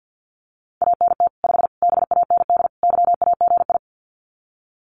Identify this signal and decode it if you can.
Morse code, heard as a keyed tone, about 50 wpm, 700 Hz. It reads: ANT 5 BAND YAGI
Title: Common QSO Phrases 50wpm